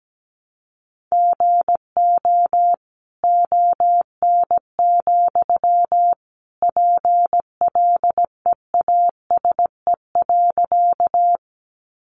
Morse code, heard as a keyed tone, about 17 wpm, 700 Hz. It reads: GO ON, PLEASE.
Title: Sentences from Top 1500 Words 17wpm